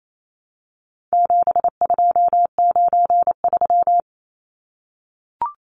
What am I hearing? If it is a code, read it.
Morse code, heard as a keyed tone, about 28 wpm, 700 Hz. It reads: 7293
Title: Numbers - 4-Digit 28wpm